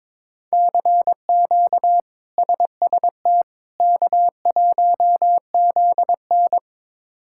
Morse code, heard as a keyed tone, about 22 wpm, 700 Hz. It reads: CQ SST K1ZN
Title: K1USN SST QSO Practice - 22wpm